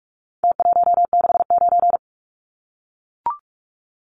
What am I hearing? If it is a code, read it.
Morse code, heard as a keyed tone, about 44 wpm, 700 Hz. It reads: T169
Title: Numbers - Contest Style - 4-Digit 45wpm